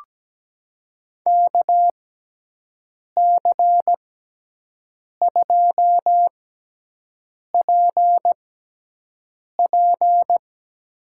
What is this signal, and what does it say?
Morse code, heard as a keyed tone, about 17 wpm, spaced as for 6 wpm, 700 Hz. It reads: KC2PP
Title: USA Call Signs - 6 Characters - 6x Character Spacing - 17wpm